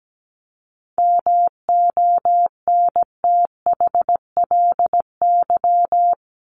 Morse code, heard as a keyed tone, about 17 wpm, 700 Hz. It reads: MONTHLY
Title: LY Suffix - Top 100 Words with Base Words 17wpm